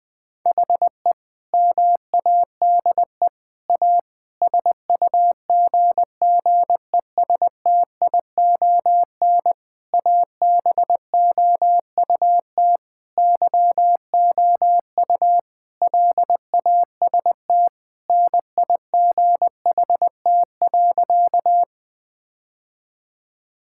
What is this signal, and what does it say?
Morse code, heard as a keyed tone, about 20 wpm, 700 Hz. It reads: HE MADE A SUGGESTION ABOUT YOU LAST NIGHT.
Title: Sentences from Top 1600 Words 20wpm